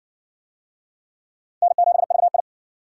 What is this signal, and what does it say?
Morse code, heard as a keyed tone, about 57 wpm, 700 Hz. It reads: N7VI